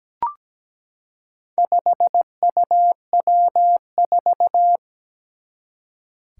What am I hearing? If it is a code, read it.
Morse code, heard as a keyed tone, about 17 wpm, 700 Hz. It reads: 5UW4
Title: Four Challenging Letter-Number Combinations 17wpm